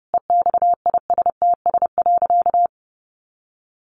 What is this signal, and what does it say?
Morse code, heard as a keyed tone, about 28 wpm, 700 Hz. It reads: EXISTS.
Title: A Sound Of Thunder 30wpm